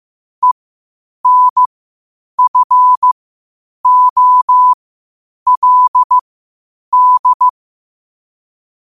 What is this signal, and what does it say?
Morse code, heard as a keyed tone, about 15 wpm, spaced as for 8 wpm, 1000 Hz. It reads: ENFOLD
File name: qotdfiles / 8wpm